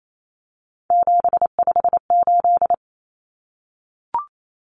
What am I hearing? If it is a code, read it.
Morse code, heard as a keyed tone, about 28 wpm, 700 Hz. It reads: 758
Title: Common Cut Numbers - 3-Digit 28wpm